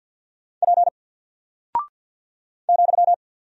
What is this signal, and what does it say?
Morse code, heard as a keyed tone, about 49 wpm, 700 Hz. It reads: P ,
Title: Single Character (Letter:Number:Punctuation:Prosign) 50wpm